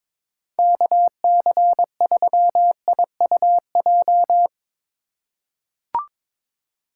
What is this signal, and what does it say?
Morse code, heard as a keyed tone, about 23 wpm, 700 Hz. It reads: KC3IUJ
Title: USA Call Signs - 6 Characters - 4x Character Spacing - 22wpm